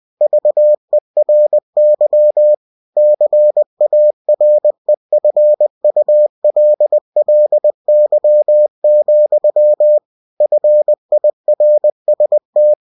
Morse code, heard as a keyed tone, about 20 wpm, 600 Hz. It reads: VERY CAREFULLY, FIRST
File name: cw-books / winnie-the-pooh / winnie-20wpm-page-0029